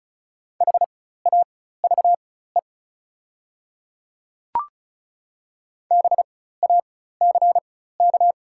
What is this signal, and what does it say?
Morse code, heard as a keyed tone, about 36 wpm, spaced as for 16 wpm, 690 Hz. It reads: HAVE BACK
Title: 4x Character Spacing - Top 200 Words - 35wpm